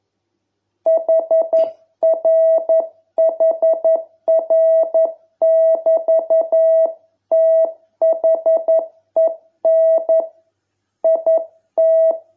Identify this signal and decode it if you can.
Morse code, heard as a keyed tone, about 11 wpm, 660 Hz. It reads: HRHR<BT>THEN IT